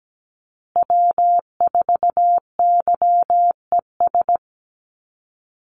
Morse code, heard as a keyed tone, about 17 wpm, 700 Hz. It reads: W4YES
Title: Call Signs Repeated 3x - 17wpm